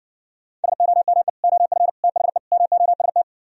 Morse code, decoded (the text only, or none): SOME OF THE MOST